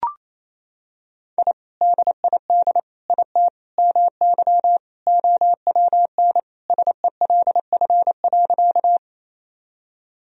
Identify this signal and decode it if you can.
Morse code, heard as a keyed tone, about 28 wpm, 700 Hz. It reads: I DID IT MY OWN SELF.
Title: Sentences from Top 1000 Words - Review Part 1 - 28wpm